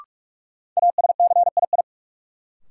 Morse code, heard as a keyed tone, about 42 wpm, 700 Hz. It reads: ASCII
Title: Sets of 5 Words - New Word Drill - 45wpm